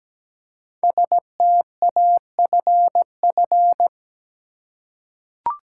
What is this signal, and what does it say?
Morse code, heard as a keyed tone, about 17 wpm, 700 Hz. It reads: STAFF
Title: Five Letter Words - 17wpm